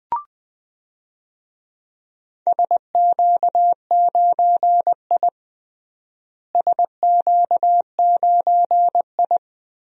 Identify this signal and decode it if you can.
Morse code, heard as a keyed tone, about 21 wpm, 700 Hz. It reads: SQ9I SQ9I
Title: DXCC-100 Call Signs - 4 Characters - Repeated 3x - 20wpm